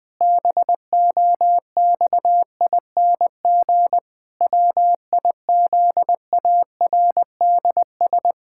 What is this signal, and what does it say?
Morse code, heard as a keyed tone, about 20 wpm, 700 Hz. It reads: BOXING WIZARDS
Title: Pangram Challenge v2 - 20wpm